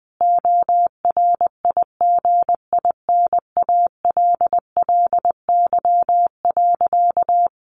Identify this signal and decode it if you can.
Morse code, heard as a keyed tone, about 20 wpm, 700 Hz. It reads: ORIGINALLY.
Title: A Sound Of Thunder 20wpm